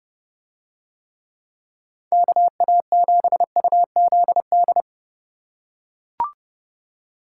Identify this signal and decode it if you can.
Morse code, heard as a keyed tone, about 29 wpm, 700 Hz. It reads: KA7UZD